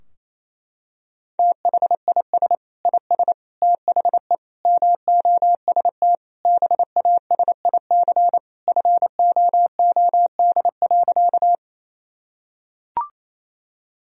Morse code, heard as a keyed tone, about 28 wpm, 700 Hz. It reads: THIS IS THE MOST BASIC FOOD.
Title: Sentences from Top 1000 Words - Review Part 3 - 28wpm